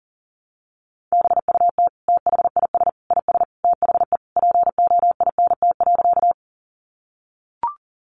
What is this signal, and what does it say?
Morse code, heard as a keyed tone, about 37 wpm, 700 Hz. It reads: BUT THIS IS THE POINT.
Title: Sentences from Top 200 Words 40wpm